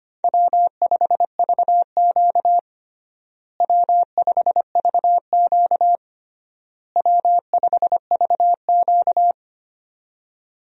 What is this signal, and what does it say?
Morse code, heard as a keyed tone, about 25 wpm, 700 Hz. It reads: W5VQ W5VQ W5VQ
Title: USA Call Signs - 4 Characters - Repeated 3x - 25wpm